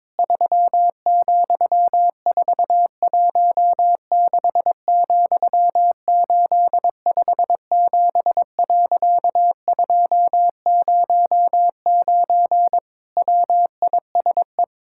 Morse code, heard as a keyed tone, about 22 wpm, 700 Hz. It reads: 3,416,857.209 WISE
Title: Pangram Challenge v2 - 22wpm